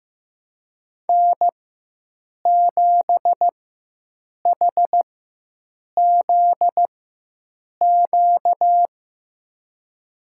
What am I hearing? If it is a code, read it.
Morse code, heard as a keyed tone, about 15 wpm, spaced as for 7 wpm, 700 Hz. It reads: N7HZQ
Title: USA Call Signs - 4x Character Spacing - 15wpm